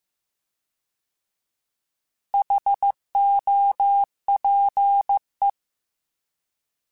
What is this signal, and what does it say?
Morse code, heard as a keyed tone, about 15 wpm, 800 Hz. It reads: HOPE
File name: Alphabetical / Alphabetical mp3s / Hs @13